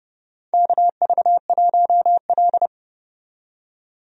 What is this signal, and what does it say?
Morse code, heard as a keyed tone, about 30 wpm, 700 Hz. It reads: KV1L
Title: USA Call Signs - 4 Characters - 30wpm